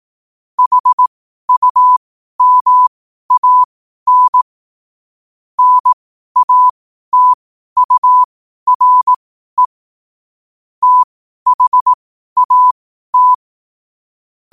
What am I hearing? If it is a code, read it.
Morse code, heard as a keyed tone, about 18 wpm, spaced as for 12 wpm, 1000 Hz. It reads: HUMAN NATURE THAT
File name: qotdfiles / 12wpm